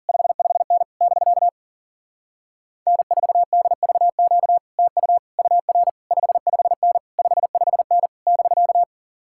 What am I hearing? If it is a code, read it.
Morse code, heard as a keyed tone, about 47 wpm, 700 Hz. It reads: 55N <BK> N4DVQ TU UR 55N 55N <BK>
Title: Speed-Racing Common QSO Phrases Encore - 40wpm